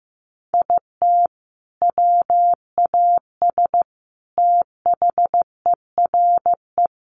Morse code, heard as a keyed tone, about 15 wpm, 700 Hz. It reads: IT WAS THERE
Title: Sentences from Top 1500 Words 15wpm